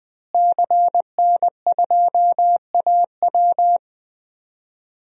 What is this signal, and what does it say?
Morse code, heard as a keyed tone, about 21 wpm, 700 Hz. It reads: CN2AW
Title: DXCC-100 Call Signs - 5 Characters - Repeated 2x - 20wpm